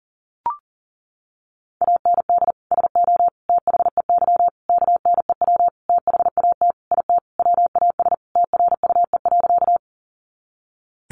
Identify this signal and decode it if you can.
Morse code, heard as a keyed tone, about 40 wpm, 700 Hz. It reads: AND SO THEY KNEW THAT IT WAS TRUE.
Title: Sentences from Top 400 Words 40wpm